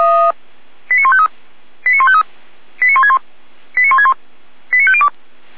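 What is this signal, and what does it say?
Five-tone selective call (ZVEI), heard as quick runs of five tones, tones of about 70 ms. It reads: 89153 89153 89161 89161 89502
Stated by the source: ZVEI five-tone selective call 89170 received by SDR.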